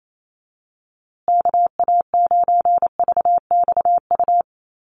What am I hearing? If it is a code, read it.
Morse code, heard as a keyed tone, about 28 wpm, 700 Hz. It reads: KA9VXU
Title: USA Call Signs Repeated 2x 28wpm